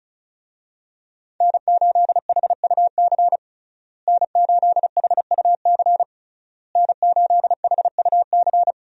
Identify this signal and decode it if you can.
Morse code, heard as a keyed tone, about 36 wpm, 700 Hz. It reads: N8HUC N8HUC N8HUC
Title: USA Call Signs - 5 Characters - Repeated 3x - 35wpm